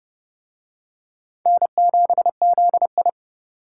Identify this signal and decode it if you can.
Morse code, heard as a keyed tone, about 29 wpm, 700 Hz. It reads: N7ZI